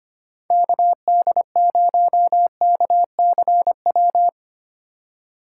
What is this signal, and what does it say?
Morse code, heard as a keyed tone, about 31 wpm, 700 Hz. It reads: KD0KCW
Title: Speed Racing - USA 2x3 Call Signs 25wpm